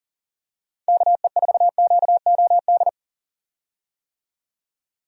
Morse code, heard as a keyed tone, about 40 wpm, 700 Hz. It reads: KE4QOD
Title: USA Call Signs - 6 Characters - 40wpm